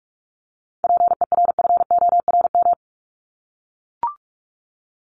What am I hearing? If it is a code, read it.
Morse code, heard as a keyed tone, about 44 wpm, 700 Hz. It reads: PERFORM
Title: ED Suffix - Top 100 Words with Base Words 45wpm